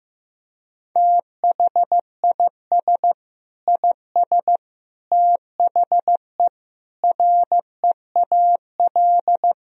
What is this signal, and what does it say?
Morse code, heard as a keyed tone, about 15 wpm, 700 Hz. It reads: THIS IS THE REAL
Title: Sentences from Top 1000 Words - Review Part 2 - 15wpm